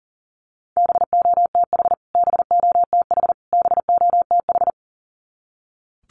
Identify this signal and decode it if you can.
Morse code, heard as a keyed tone, about 40 wpm, 700 Hz. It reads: BOTH BOTH BOTH
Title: Top 500 Words Repeated 40wpm